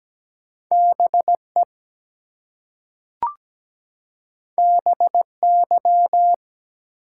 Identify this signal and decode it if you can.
Morse code, heard as a keyed tone, about 17 wpm, 700 Hz. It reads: BE BY
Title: Two Letter Words - 17wpm